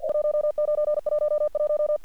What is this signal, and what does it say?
Morse code, heard as a keyed tone, about 48 wpm, 600 Hz. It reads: 1911
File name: archived / CRSnet